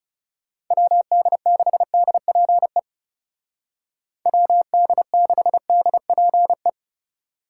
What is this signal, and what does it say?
Morse code, heard as a keyed tone, about 40 wpm, 700 Hz. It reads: WD6DPE WD6DPE